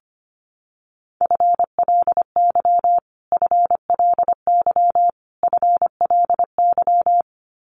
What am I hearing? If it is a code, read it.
Morse code, heard as a keyed tone, about 25 wpm, 700 Hz. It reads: FLY FLY FLY
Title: Top 500 Words Repeated 25wpm